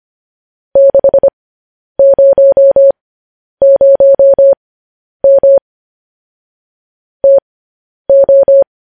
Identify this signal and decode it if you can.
Morse code, heard as a keyed tone, about 25 wpm, spaced as for 10 wpm, 550 Hz. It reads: 600M TO
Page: news headlines in morse code at 10 wpm